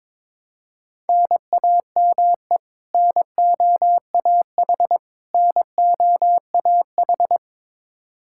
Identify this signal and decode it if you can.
Morse code, heard as a keyed tone, about 21 wpm, 700 Hz. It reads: NAME NOAH NOAH
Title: Practice QSO - Exchange 1 - Beginning - 22wpm